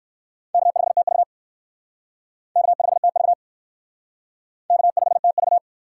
Morse code, heard as a keyed tone, about 58 wpm, 700 Hz. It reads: K5TV K5TV K5TV